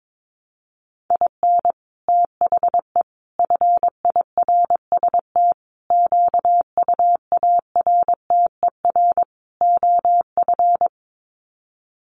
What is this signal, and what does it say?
Morse code, heard as a keyed tone, about 22 wpm, 700 Hz. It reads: IN THE FIRST QUARTER OF
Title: Sets of 5 Words Encore2 22wpm